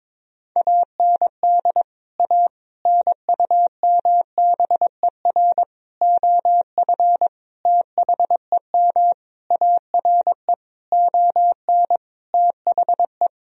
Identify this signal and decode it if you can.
Morse code, heard as a keyed tone, about 22 wpm, 700 Hz. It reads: AND A NUMBER OF THEM ARE ON THE
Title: Sentences from Top 1000 Words - Review Part 1 - 22wpm